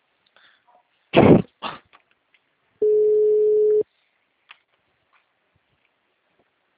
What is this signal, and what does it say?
Ringback tone (Europe, 425 Hz).